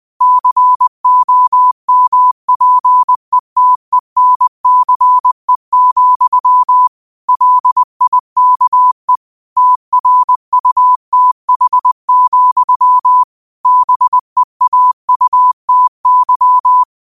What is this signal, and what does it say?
Morse code, heard as a keyed tone, about 20 wpm, 1000 Hz. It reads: COMPETENCE, LIKE TRUTH, BEAUTY
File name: qotdfiles / 20wpm